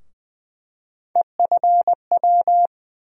Morse code, heard as a keyed tone, about 20 wpm, 700 Hz. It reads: EFW
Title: USA Call Sign Postfixes - 3 Letters 20wpm